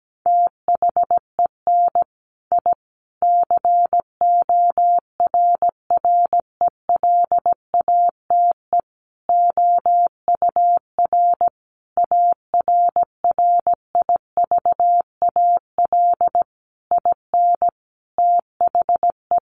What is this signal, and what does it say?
Morse code, heard as a keyed tone, about 17 wpm, 700 Hz. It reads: THEN I CORRELATE OUR ARRIVAL IN THE
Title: A Sound Of Thunder 17wpm